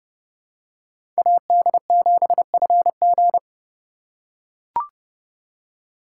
Morse code, heard as a keyed tone, about 30 wpm, 700 Hz. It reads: AD7FG
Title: USA Call Signs - 5 Characters - 3x Character Spacing - 30wpm